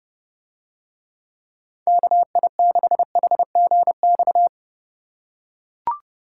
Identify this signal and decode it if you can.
Morse code, heard as a keyed tone, about 29 wpm, 700 Hz. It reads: KI6HGX